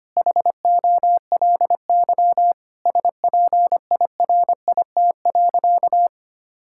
Morse code, heard as a keyed tone, about 25 wpm, 700 Hz. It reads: HOLY SPIRIT.
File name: practice / 25wpm / Sentences with 9-Word Phrases - 25wpm